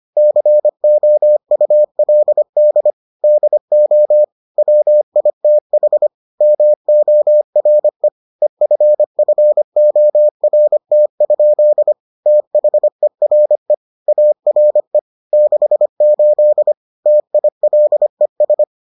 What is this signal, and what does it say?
Morse code, heard as a keyed tone, about 25 wpm, 600 Hz. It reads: COULD DO WITH MORE EFFORT? THERE ARE 68 TILES